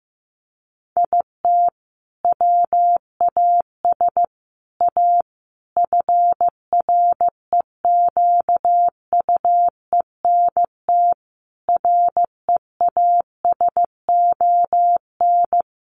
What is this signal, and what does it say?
Morse code, heard as a keyed tone, about 15 wpm, 700 Hz. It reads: IT WAS A FREQUENT REASON
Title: Sentences from Top 1400 Words 15wpm